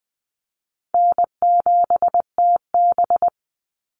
Morse code, heard as a keyed tone, about 20 wpm, 700 Hz. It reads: N7TB
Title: Easy Call Signs Repeated 3x - 20wpm